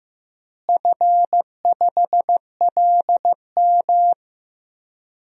Morse code, heard as a keyed tone, about 15 wpm, 700 Hz. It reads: F5LM
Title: DXCC-100 Call Signs - 4 Characters - Repeated 3x - 15wpm